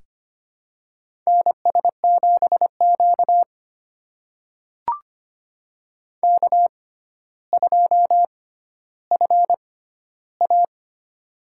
Morse code, heard as a keyed tone, about 25 wpm, spaced as for 13 wpm, 700 Hz. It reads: NS7Q K2FA
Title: USA Call Signs - 4 Characters - 6x Character Spacing - 25wpm